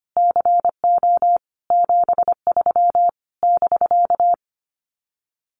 Morse code, heard as a keyed tone, about 25 wpm, 700 Hz. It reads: CO 73 <BK>
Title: Common QSO Phrases Encore 25wpm